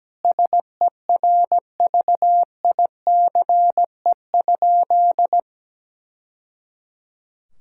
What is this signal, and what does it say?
Morse code, heard as a keyed tone, about 17 wpm, 700 Hz. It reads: SERVICE?
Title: Sentences from Top 1000 Words - Review Part 5 - 17wpm